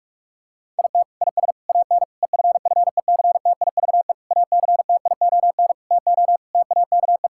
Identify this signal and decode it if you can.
Morse code, heard as a keyed tone, about 45 wpm, 700 Hz. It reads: IT IS AN EFFECTIVE ACTION TO TAKE
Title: Sentences from Top 1000 Words - Review Part 6 - 45wpm